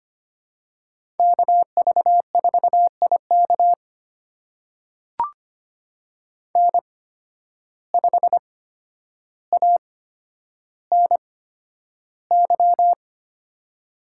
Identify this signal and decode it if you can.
Morse code, heard as a keyed tone, about 25 wpm, spaced as for 11 wpm, 700 Hz. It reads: KV4IK N5ANY
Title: USA Call Signs - 5 Characters - 8x Character Spacing - 25wpm